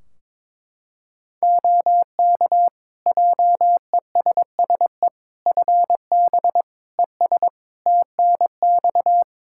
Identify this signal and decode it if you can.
Morse code, heard as a keyed tone, about 22 wpm, 700 Hz. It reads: OK JESSE FB ES TNX
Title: Practice QSO - Exchange 2 - Info - 22wpm